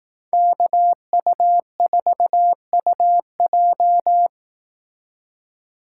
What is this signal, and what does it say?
Morse code, heard as a keyed tone, about 18 wpm, 700 Hz. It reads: KU4UJ
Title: Speed Racing - USA 2x2 Call Signs 15wpm